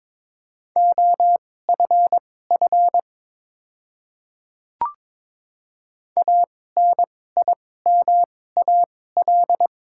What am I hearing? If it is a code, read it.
Morse code, heard as a keyed tone, about 22 wpm, spaced as for 16 wpm, 700 Hz. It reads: OFF ANIMAL
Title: 2x Character Spacing - Top 200 Words - 22wpm